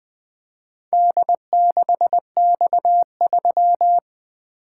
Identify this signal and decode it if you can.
Morse code, heard as a keyed tone, about 20 wpm, 700 Hz. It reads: D6X3
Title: Four Letter-Number Combinations - 20wpm